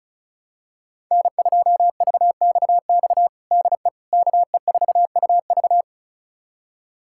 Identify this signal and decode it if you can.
Morse code, heard as a keyed tone, about 36 wpm, 700 Hz. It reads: N2VXX DE KE4UV
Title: USA Callsign DE Callsign - 5 Characters - 35wpm